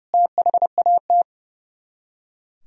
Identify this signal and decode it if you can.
Morse code, heard as a keyed tone, about 37 wpm, 700 Hz. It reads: THAT
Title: Speed-Racing - Sets of 2 Words 30wpm